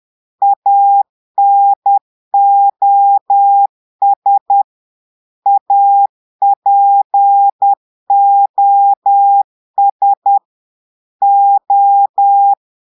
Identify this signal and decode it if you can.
Morse code, heard as a keyed tone, about 10 wpm, 800 Hz. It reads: ANOS APOS O
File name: Portugais - texte en clair 1 - vitesse - 10 mots minute